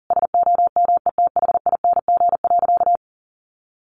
Morse code, heard as a keyed tone, about 40 wpm, 700 Hz. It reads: SOMETHING.
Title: Sentences from Top 1400 Words 40wpm